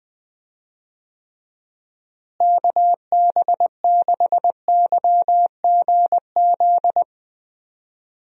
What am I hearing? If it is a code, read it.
Morse code, heard as a keyed tone, about 20 wpm, 700 Hz. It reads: KB6YGZ
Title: USA Call Signs - 6 Characters - Repeated 2x - 20wpm